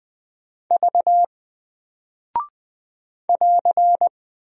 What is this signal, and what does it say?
Morse code, heard as a keyed tone, about 20 wpm, 700 Hz. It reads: V <AR>
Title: Single Character (Letter:Number:Punctuation:Prosign) Alt 20wpm